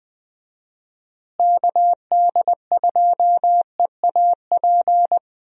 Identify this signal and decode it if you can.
Morse code, heard as a keyed tone, about 21 wpm, 700 Hz. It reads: KD2EAP
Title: USA Call Signs - 6 Characters - Repeated 2x - 20wpm